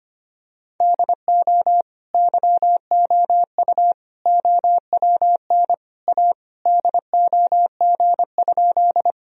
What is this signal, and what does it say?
Morse code, heard as a keyed tone, about 25 wpm, 700 Hz. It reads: DO YOU OWN A DOG?
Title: Sentences from Top 1000 Words - Review Part 2 - 25wpm